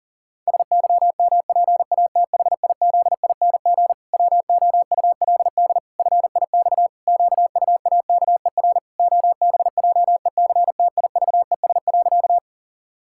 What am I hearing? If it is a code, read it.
Morse code, heard as a keyed tone, about 37 wpm, 700 Hz. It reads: SYMPATHIZING WOULD FIX QUAKER OBJECTIVES.
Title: Pangram Challenge v2 - 40wpm